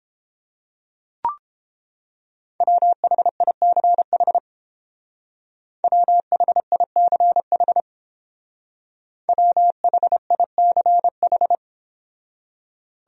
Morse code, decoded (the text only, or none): WHICH WHICH WHICH